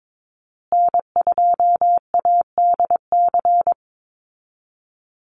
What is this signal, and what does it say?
Morse code, heard as a keyed tone, about 22 wpm, 700 Hz. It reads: N2ADC
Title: Call Signs Repeated 3x - 22wpm